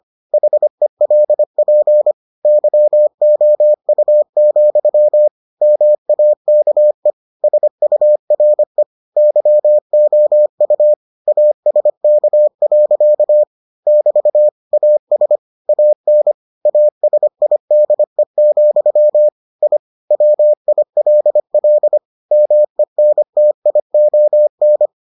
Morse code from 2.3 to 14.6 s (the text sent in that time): YOU, MAKE SURE YOU ASK. <BT>